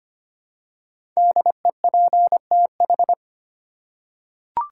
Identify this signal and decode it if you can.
Morse code, heard as a keyed tone, about 25 wpm, 700 Hz. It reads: DEPTH
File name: 5 Letter Words - First 3 Letters Given - 25wpm